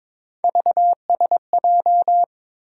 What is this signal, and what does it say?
Morse code, heard as a keyed tone, about 22 wpm, 700 Hz. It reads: VSJ
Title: USA Call Sign Postfixes - 3 Letters 22wpm